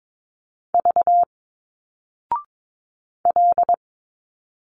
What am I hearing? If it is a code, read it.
Morse code, heard as a keyed tone, about 22 wpm, 700 Hz. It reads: V L
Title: Single Letters 22wpm